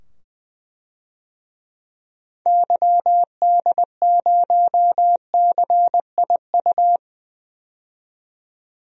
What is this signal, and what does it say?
Morse code, heard as a keyed tone, about 20 wpm, 700 Hz. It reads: YD0CIU
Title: DXCC-100 Call Signs - 6 Characters - 20wpm